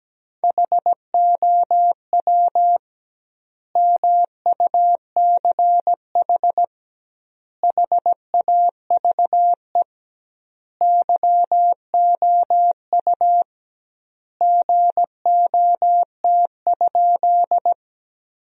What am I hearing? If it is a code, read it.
Morse code, heard as a keyed tone, about 17 wpm, 700 Hz. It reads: HOW MUCH HAVE YOU GOT?
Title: 2x Word-Spacing - Sentences from Top 300 Words - 17wpm